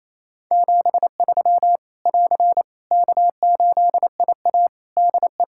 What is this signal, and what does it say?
Morse code, heard as a keyed tone, about 28 wpm, 700 Hz. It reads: 73 <AR> K8IA DE
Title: Practice QSO - Exchange 4 - Ending - 28wpm